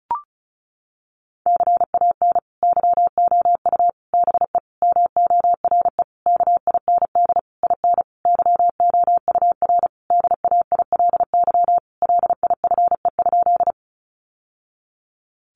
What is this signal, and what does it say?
Morse code, heard as a keyed tone, about 34 wpm, 700 Hz. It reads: CAN YOU BE MORE KIND IN YOUR DAILY LIFE?
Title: Sentences from Top 1100 Words 35wpm